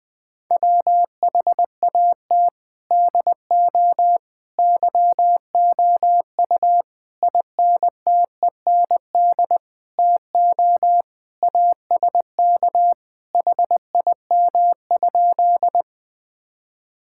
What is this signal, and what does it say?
Morse code, heard as a keyed tone, about 20 wpm, 700 Hz. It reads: WHAT DO YOU INTEND TO ASK HIM?
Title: Sentences from Top 1600 Words 20wpm